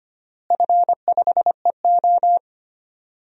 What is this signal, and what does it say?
Morse code, heard as a keyed tone, about 25 wpm, 700 Hz. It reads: F5EO
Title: DXCC-100 Call Signs - 4 Characters - Repeated 3x - 25wpm